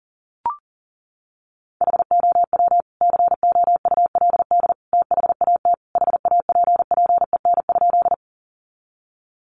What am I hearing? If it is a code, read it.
Morse code, heard as a keyed tone, about 40 wpm, 700 Hz. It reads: HOW COULD THAT HAPPEN?
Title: Sentences from Top 400 Words 40wpm